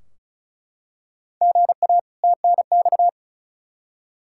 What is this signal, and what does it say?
Morse code, heard as a keyed tone, about 36 wpm, 700 Hz. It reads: GA TNX
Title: Practice QSO - Exchange 1 - Beginning - 35wpm